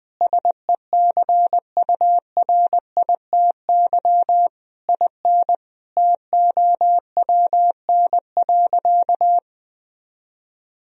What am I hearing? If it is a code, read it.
Morse code, heard as a keyed tone, about 20 wpm, 700 Hz. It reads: SECURITY IN TOWN.
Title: Sentences from Top 1000 Words - Review Part 4 - 20wpm